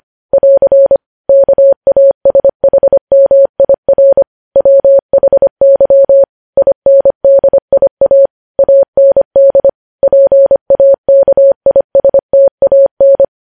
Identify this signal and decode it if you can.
Morse code, heard as a keyed tone, about 25 wpm, 550 Hz. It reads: <AR> KASHMIR WHY INDIA AND PAKISTAN
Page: news headlines in morse code at 25 wpm